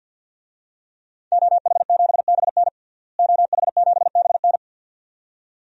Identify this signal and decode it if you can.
Morse code, heard as a keyed tone, about 46 wpm, 700 Hz. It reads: OH7BN OH7BN
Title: DXCC-100 Call Signs - 5 Characters - Repeated 2x - 50wpm